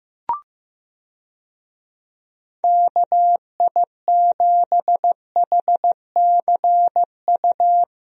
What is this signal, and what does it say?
Morse code, heard as a keyed tone, about 15 wpm, 700 Hz. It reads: KI7HCU
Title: USA Call Signs - 6 Characters - 15wpm